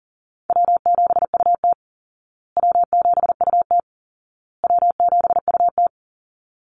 Morse code, heard as a keyed tone, about 40 wpm, 700 Hz. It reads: W7UT W7UT W7UT
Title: Easy Call Signs Repeated 3x - 40wpm